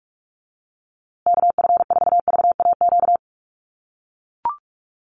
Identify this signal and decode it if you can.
Morse code, heard as a keyed tone, about 44 wpm, 700 Hz. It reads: KF4VAQ
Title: USA Call Signs Repeated 3x 45wpm